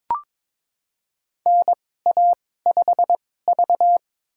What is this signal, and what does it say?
Morse code, heard as a keyed tone, about 22 wpm, spaced as for 16 wpm, 700 Hz. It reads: NA5V
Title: USA Call Signs - 2x Character Spacing - 22wpm